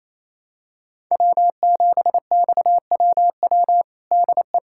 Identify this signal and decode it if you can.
Morse code, heard as a keyed tone, about 28 wpm, 700 Hz. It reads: W7XWW DE
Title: USA Callsign DE Callsign - 5 Characters - 28wpm